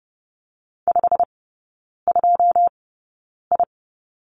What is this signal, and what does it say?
Morse code, heard as a keyed tone, about 30 wpm, spaced as for 9 wpm, 700 Hz. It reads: 52I
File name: Lesson 17e - Letter u - Three Characters 30wpm